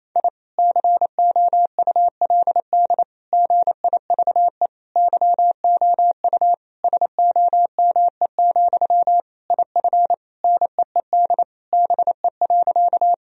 Morse code, heard as a keyed tone, about 28 wpm, 700 Hz. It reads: I COULD GIVE YOU SOME, IF NEED BE.
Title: Sentences from Top 1000 Words - Review Part 1 - 28wpm